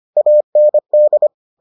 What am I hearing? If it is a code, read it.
Morse code, heard as a keyed tone, about 25 wpm, 600 Hz. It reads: AND